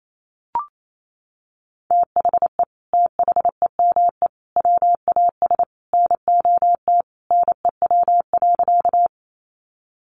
Morse code, heard as a keyed tone, about 28 wpm, 700 Hz. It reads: THE THEME WAS NOT NEW.
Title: Sentences from Top 1100 Words 28wpm